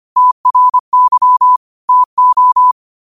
Morse code, heard as a keyed tone, about 25 wpm, 1000 Hz. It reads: TRY TO